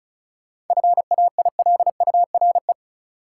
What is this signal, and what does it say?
Morse code, heard as a keyed tone, about 36 wpm, 700 Hz. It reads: FAILURE
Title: Sentences from Top 900 Words - New Word Drill - 35wpm